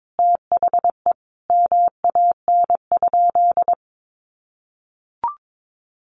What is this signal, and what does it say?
Morse code, heard as a keyed tone, about 22 wpm, 700 Hz. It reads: THE MAN?
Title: Sentences from Top 1300 Words 22wpm